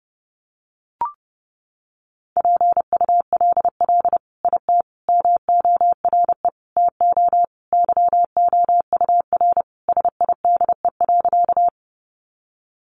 Morse code, heard as a keyed tone, about 30 wpm, 700 Hz. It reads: PULL IT MORE TO YOUR SIDE.
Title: Sentences from Top 400 Words 30wpm